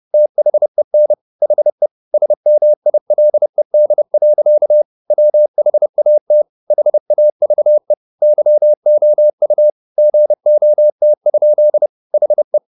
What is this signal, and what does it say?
Morse code, heard as a keyed tone, about 30 wpm, 600 Hz. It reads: THEN HE SMILED. WHAT HAVE YOU GOT? HE
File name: the old man and the sea / the old man and the sea 1 30wpm